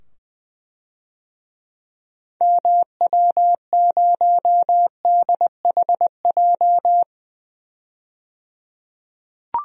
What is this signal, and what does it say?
Morse code, heard as a keyed tone, about 20 wpm, 700 Hz. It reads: MW0DHJ
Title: DXCC-100 Call Signs - 6 Characters - 20wpm